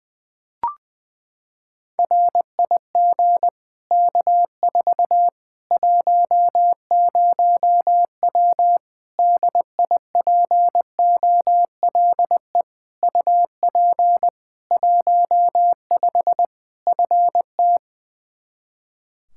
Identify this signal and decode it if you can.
Morse code, heard as a keyed tone, about 20 wpm, 700 Hz. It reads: RIG K4 10W DIPOLE UP 15 FT
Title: Practice QSO - Exchange 2 - Info - 20wpm